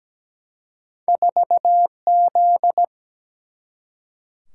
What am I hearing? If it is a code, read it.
Morse code, heard as a keyed tone, about 17 wpm, 700 Hz. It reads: 4Z
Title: Single-Letter Single-Number 17wpm